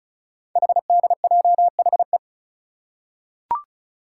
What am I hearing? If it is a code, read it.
Morse code, heard as a keyed tone, about 36 wpm, 690 Hz. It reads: HDJHE
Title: Code Groups - 5 Letters - 35wpm